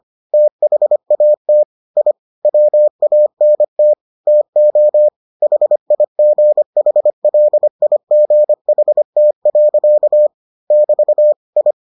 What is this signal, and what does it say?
Morse code, heard as a keyed tone, about 25 wpm, 600 Hz. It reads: THAT I WANT TO HIGHLIGHT. <BT> I